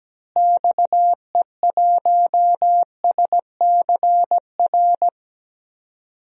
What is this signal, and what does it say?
Morse code, heard as a keyed tone, about 17 wpm, 700 Hz. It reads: XE1SCR
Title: DXCC-100 Call Signs - 6 Characters - Repeated 3x - 17wpm